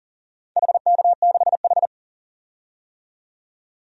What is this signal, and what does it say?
Morse code, heard as a keyed tone, about 40 wpm, 700 Hz. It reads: HK6H
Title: DXCC-100 Call Signs - 4 Characters - 40wpm